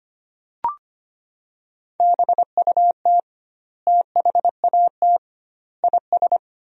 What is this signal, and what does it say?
Morse code, heard as a keyed tone, about 25 wpm, 700 Hz. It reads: BUT THAT IS
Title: 2x Word-Spacing - Sentences from Top 500 Words - 25wpm